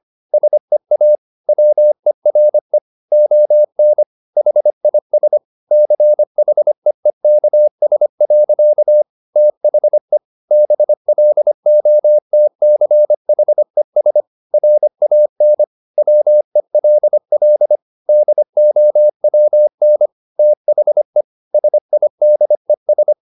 Morse code, heard as a keyed tone, about 25 wpm, 600 Hz. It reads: SEA WERE ON HIS CHEEKS. THE BLOTCHES RAN WELL DOWN THE SIDES
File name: the old man and the sea part 1 25wpm